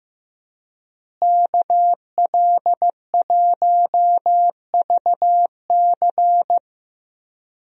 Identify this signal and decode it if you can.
Morse code, heard as a keyed tone, about 15 wpm, 700 Hz. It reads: KL1VC
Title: USA Call Signs - 5 Characters - Repeated 2x - 15wpm